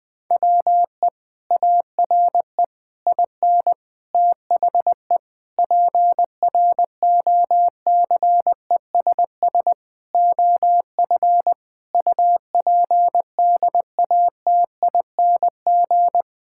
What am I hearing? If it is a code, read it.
Morse code, heard as a keyed tone, about 20 wpm, 700 Hz. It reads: WE ARE IN THE PROCESS OF UPDATING
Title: Sentences with 6-Word Phrases - Spoken Beginning - 20wpm